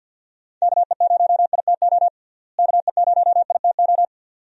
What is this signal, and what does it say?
Morse code, heard as a keyed tone, about 53 wpm, 700 Hz. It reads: KE0ITO KE0ITO
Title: USA Call Signs - 6 Characters - Repeated 2x - 50wpm